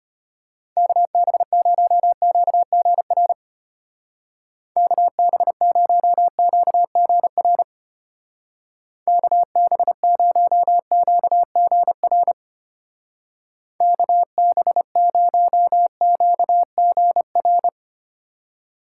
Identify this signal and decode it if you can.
Morse code, heard as a keyed tone, about 32 wpm, 700 Hz. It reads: KB0QGR KB0QGR KB0QGR KB0QGR